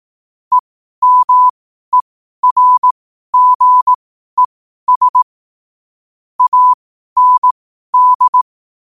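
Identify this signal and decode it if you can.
Morse code, heard as a keyed tone, about 18 wpm, spaced as for 12 wpm, 1000 Hz. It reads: EMERGES AND
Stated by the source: Morse code at 12 wpm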